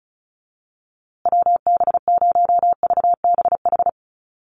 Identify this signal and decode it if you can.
Morse code, heard as a keyed tone, about 34 wpm, 700 Hz. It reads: WB0VBH
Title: USA Call Signs Repeated 3x 35wpm